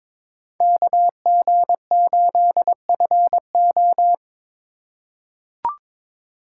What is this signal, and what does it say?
Morse code, heard as a keyed tone, about 21 wpm, 700 Hz. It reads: KG8FO
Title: USA Call Signs - 5 Characters - 4x Character Spacing - 22wpm